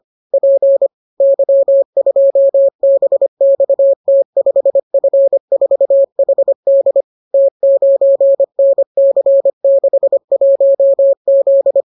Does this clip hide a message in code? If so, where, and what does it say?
Yes: Morse code from 0.3 to 11.8 s, heard as a keyed tone, about 25 wpm, 550 Hz. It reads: P Y2BXT5F4HD T9NC61Z